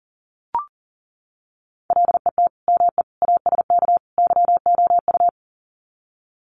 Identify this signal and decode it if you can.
Morse code, heard as a keyed tone, about 40 wpm, 700 Hz. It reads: LET ME ASK YOU
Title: Sets of 4 Words 40wpm